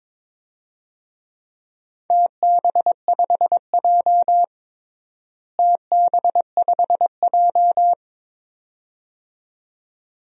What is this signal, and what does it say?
Morse code, heard as a keyed tone, about 21 wpm, 700 Hz. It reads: TB5J TB5J
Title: DXCC-100 Call Signs - 4 Characters - Repeated 2x - 22wpm